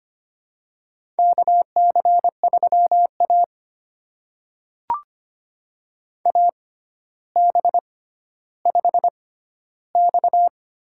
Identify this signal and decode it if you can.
Morse code, heard as a keyed tone, about 25 wpm, spaced as for 13 wpm, 700 Hz. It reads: KC3A AB5X
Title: USA Call Signs - 4 Characters - 6x Character Spacing - 25wpm